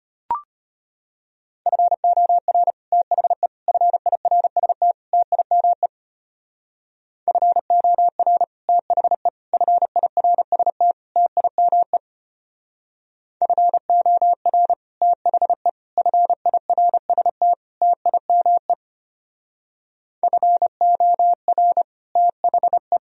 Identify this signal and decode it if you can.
Morse code, heard as a keyed tone, about 33 wpm, 700 Hz. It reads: FOR THE FIRST TIME FOR THE FIRST TIME FOR THE FIRST TIME FOR THE